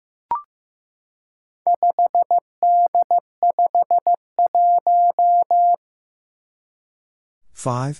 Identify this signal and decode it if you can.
Morse code, heard as a keyed tone, about 15 wpm, 700 Hz. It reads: 5D51
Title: Four Challenging Letter-Number Combinations 15wpm